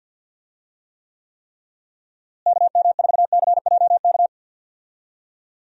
Morse code, heard as a keyed tone, about 52 wpm, 700 Hz. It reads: KM4CJK
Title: USA Call Signs - 6 Characters - Repeated 3x - 50wpm